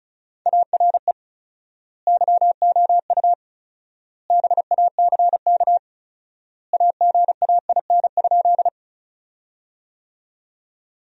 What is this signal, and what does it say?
Morse code, heard as a keyed tone, about 36 wpm, 700 Hz. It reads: ARE YOU BACK AGAIN?
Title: 4x Word-Spacing - Sentences from Top 200 Words - 35wpm